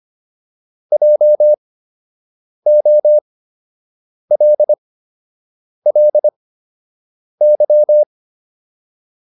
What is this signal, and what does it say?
Morse code, heard as a keyed tone, about 25 wpm, spaced as for 7 wpm, 600 Hz. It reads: JOLLY